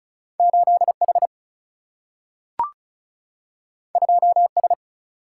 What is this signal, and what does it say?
Morse code, heard as a keyed tone, about 36 wpm, 700 Hz. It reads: 8H 2S
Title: Single-Letter Single-Number 35wpm